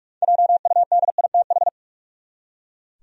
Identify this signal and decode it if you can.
Morse code, heard as a keyed tone, about 45 wpm, 700 Hz. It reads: JUDITH
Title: USA Common Names - 45wpm